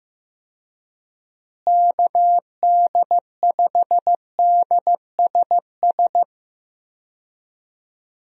Morse code, heard as a keyed tone, about 14 wpm, 700 Hz. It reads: KD5DSS
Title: USA Call Signs - 6 Characters - Repeated 2x - 15wpm